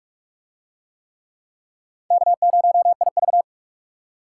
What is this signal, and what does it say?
Morse code, heard as a keyed tone, about 42 wpm, 700 Hz. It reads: K0IV